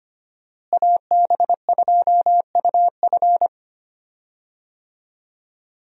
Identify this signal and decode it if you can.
Morse code, heard as a keyed tone, about 25 wpm, 700 Hz. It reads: AB2UF
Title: USA Call Signs - 5 Characters - Repeated 2x - 25wpm